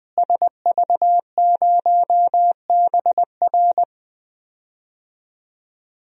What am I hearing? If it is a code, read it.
Morse code, heard as a keyed tone, about 19 wpm, 700 Hz. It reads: SV0BR
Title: DXCC-100 Call Signs - 5 Characters - Repeated 3x - 20wpm